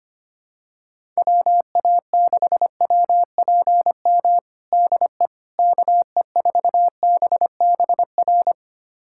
Morse code, heard as a keyed tone, about 25 wpm, 700 Hz. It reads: WA6WPM DE KE4BBR
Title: USA Callsign DE Callsign - 6 Characters - 25wpm